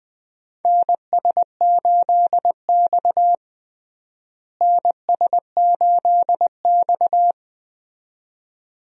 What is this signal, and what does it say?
Morse code, heard as a keyed tone, about 20 wpm, 700 Hz. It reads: NS8X NS8X
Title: USA Call Signs - 4 Characters - Repeated 2x - 20wpm